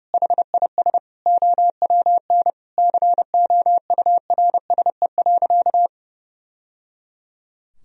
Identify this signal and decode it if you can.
Morse code, heard as a keyed tone, about 30 wpm, 700 Hz. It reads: HIS OWN COURSE.
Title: Sentences from Top 1000 Words - Review Part 1 - 30wpm